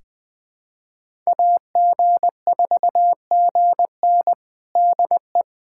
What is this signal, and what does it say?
Morse code, heard as a keyed tone, about 20 wpm, 700 Hz. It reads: AG4GN DE
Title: USA Callsign DE Callsign - 5 Characters - 20wpm